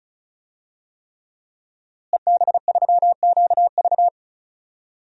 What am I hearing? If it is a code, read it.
Morse code, heard as a keyed tone, about 34 wpm, 700 Hz. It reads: EB3QV
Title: DXCC-100 Call Signs - 5 Characters - Repeated 3x - 35wpm